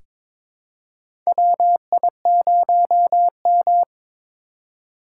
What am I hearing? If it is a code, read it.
Morse code, heard as a keyed tone, about 22 wpm, 700 Hz. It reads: WI0M
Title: USA Call Signs - 4 Characters - 4x Character Spacing - 22wpm